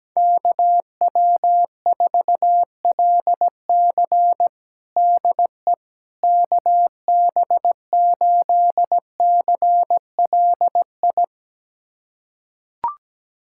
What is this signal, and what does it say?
Morse code, heard as a keyed tone, about 17 wpm, 700 Hz. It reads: KW4LC DE KB8CLI
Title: USA Callsign DE Callsign 17wpm